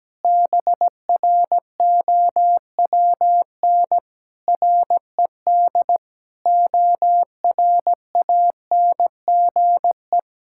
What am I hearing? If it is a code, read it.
Morse code, heard as a keyed tone, about 17 wpm, 700 Hz. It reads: BROWN RED ORANGE
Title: Pangram Challenge v2 - 17wpm